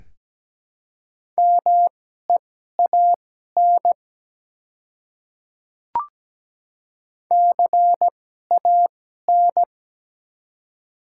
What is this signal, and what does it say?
Morse code, heard as a keyed tone, about 17 wpm, spaced as for 12 wpm, 700 Hz. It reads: MEAN CAN
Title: 2x Character Spacing - Top 200 Words - 17wpm